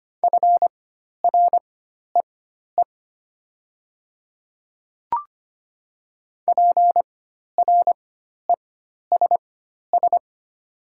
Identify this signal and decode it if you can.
Morse code, heard as a keyed tone, about 25 wpm, spaced as for 12 wpm, 700 Hz. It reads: FREE PRESS
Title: 4x Character Spacing - Top 500 Words - 25wpm